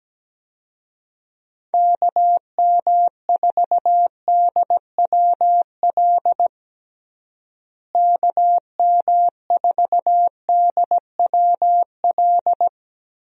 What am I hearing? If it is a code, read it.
Morse code, heard as a keyed tone, about 17 wpm, 700 Hz. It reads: KM4DWL KM4DWL
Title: USA Call Signs - 6 Characters - Repeated 2x - 17wpm